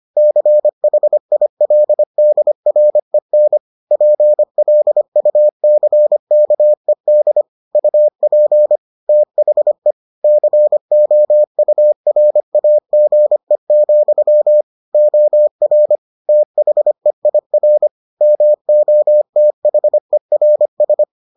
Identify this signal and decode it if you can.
Morse code, heard as a keyed tone, about 25 wpm, 600 Hz. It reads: CHILDREN PLUCKED UP THE COURAGE, OR THEIR MOTHERS